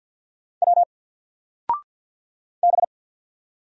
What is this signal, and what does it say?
Morse code, heard as a keyed tone, about 50 wpm, 700 Hz. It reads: W B
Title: Single Character (Letter:Number:Punctuation:Prosign) 50wpm